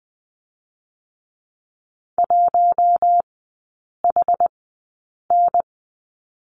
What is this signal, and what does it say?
Morse code, heard as a keyed tone, about 20 wpm, spaced as for 8 wpm, 700 Hz. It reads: 1HN
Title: 060 Lesson 11e - Letter h - Three Characters 20wpm